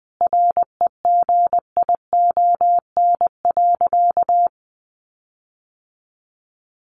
Morse code, heard as a keyed tone, about 19 wpm, 700 Hz. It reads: REGION.
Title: Sentences from Top 1500 Words 20wpm